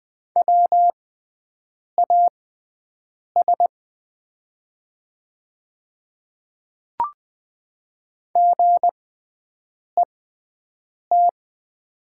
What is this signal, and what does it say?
Morse code, heard as a keyed tone, about 20 wpm, spaced as for 7 wpm, 700 Hz. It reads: WAS GET
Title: 6x Character Spacing - Top 200 Words - 20wpm